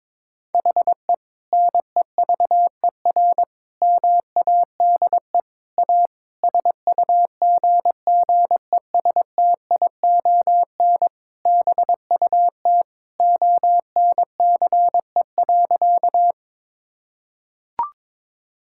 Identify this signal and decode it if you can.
Morse code, heard as a keyed tone, about 21 wpm, 700 Hz. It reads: HE NEVER MADE A SUGGESTION BUT ONCE.
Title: Sentences from Top 1600 Words 22wpm